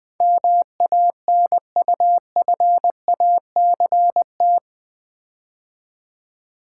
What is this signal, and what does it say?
Morse code, heard as a keyed tone, about 20 wpm, 700 Hz. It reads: MANUFACT
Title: Build A Word - 4 Letters At A Time - 11-Letter Words - 20wpm